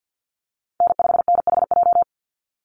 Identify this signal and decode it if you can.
Morse code, heard as a keyed tone, about 50 wpm, 700 Hz. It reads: N5NHJ
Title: Call Signs - 50wpm